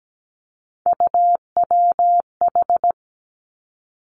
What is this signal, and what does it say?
Morse code, heard as a keyed tone, about 17 wpm, 700 Hz. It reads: UWH
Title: Three Challenging Letter-Number Combinations 17wpm